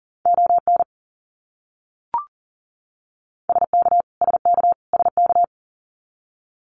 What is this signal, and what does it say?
Morse code, heard as a keyed tone, about 40 wpm, 700 Hz. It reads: ON SK SK SK
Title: Canadian Provinces and Territories - Repeated 3x - 40wpm